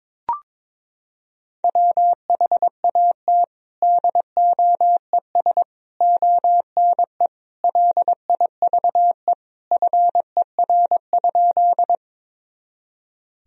Morse code, heard as a keyed tone, about 22 wpm, 700 Hz. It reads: WHAT DOES ONE LIVE FER?
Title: Sentences from Top 200 Words and Abbreviations 22wpm